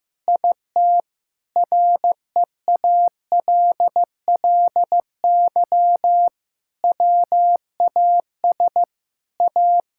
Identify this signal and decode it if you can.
Morse code, heard as a keyed tone, about 15 wpm, 700 Hz. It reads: IT REALLY WAS A
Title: Sentences from Top 1600 Words 15wpm